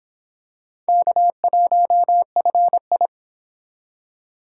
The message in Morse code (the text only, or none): K1FI